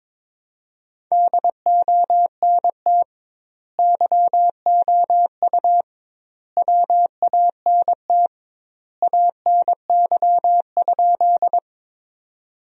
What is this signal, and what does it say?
Morse code, heard as a keyed tone, about 22 wpm, 700 Hz. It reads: DONT YOU WANT ANY?
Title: 2x Word-Spacing - Sentences from Top 300 Words - 22wpm